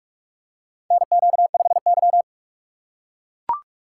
Morse code, heard as a keyed tone, about 44 wpm, 690 Hz. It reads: NQ5Y
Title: USA Call Signs - 4 Characters - 2x Character Spacing - 45wpm